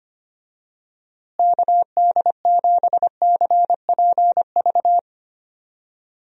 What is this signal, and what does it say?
Morse code, heard as a keyed tone, about 25 wpm, 700 Hz. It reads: KD7CPV
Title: USA Call Signs - 6 Characters - Repeated 2x - 25wpm